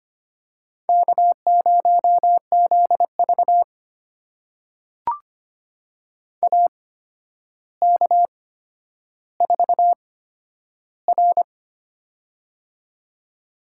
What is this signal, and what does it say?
Morse code, heard as a keyed tone, about 25 wpm, spaced as for 11 wpm, 700 Hz. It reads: K0ZV AK4R
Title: USA Call Signs - 4 Characters - 8x Character Spacing - 25wpm